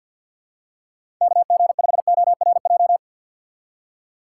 Morse code, heard as a keyed tone, about 53 wpm, 700 Hz. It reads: KG5ORJ
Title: USA Call Signs - 6 Characters - 6x Character Spacing - 50wpm